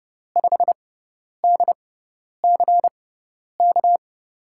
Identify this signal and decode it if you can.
Morse code, heard as a keyed tone, about 30 wpm, spaced as for 10 wpm, 700 Hz. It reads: 5DCK
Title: USA Call Signs - 6 Characters - 6x Character Spacing - 30wpm